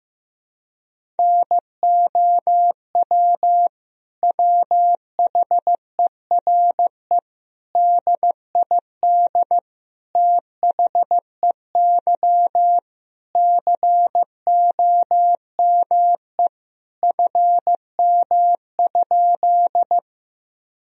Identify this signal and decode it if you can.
Morse code, heard as a keyed tone, about 15 wpm, 700 Hz. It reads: NOW WHERE DID THEY COME FM?
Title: Sentences from Top 200 Words and Abbreviations 15wpm